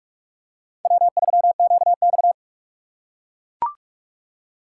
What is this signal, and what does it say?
Morse code, heard as a keyed tone, about 43 wpm, 700 Hz. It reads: W3QX
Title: USA Call Signs - 4 Characters - 3x Character Spacing - 45wpm